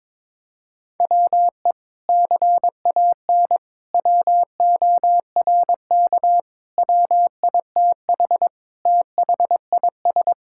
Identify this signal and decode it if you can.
Morse code, heard as a keyed tone, about 22 wpm, 700 Hz. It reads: WE CAN WORK WITH THIS
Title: Sentences from Top 1000 Words - Review Part 5 - 22wpm